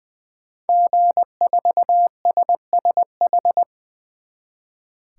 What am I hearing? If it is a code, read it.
Morse code, heard as a keyed tone, about 20 wpm, 700 Hz. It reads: G4SSH
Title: 2023 SOTA Chaser Call Signs - 20wpm